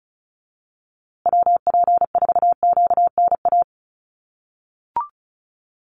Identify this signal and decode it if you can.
Morse code, heard as a keyed tone, about 34 wpm, 700 Hz. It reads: WP4QNA
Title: USA Call Signs Repeated 3x 35wpm